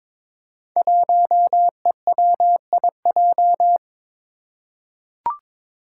Morse code, heard as a keyed tone, about 22 wpm, 700 Hz. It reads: 1EWIJ
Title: Five Challenging Letter-Number Combinations 22wpm